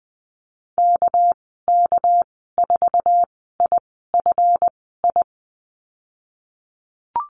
Morse code, heard as a keyed tone, about 20 wpm, spaced as for 14 wpm, 700 Hz. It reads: KK4IFI
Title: 231 Lesson 37g - Letter x - Call Signs 20wpm